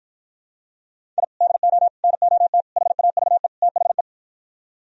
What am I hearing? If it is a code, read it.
Morse code, heard as a keyed tone, about 52 wpm, 700 Hz. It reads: I DO NOT HAVE THE